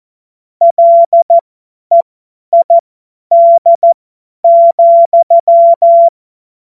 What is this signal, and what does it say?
Morse code, heard as a keyed tone, about 14 wpm, spaced as for 10 wpm, 680 Hz. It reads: LEID,
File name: Morsekurs / Witze / Witz 25. April 2023 50 F70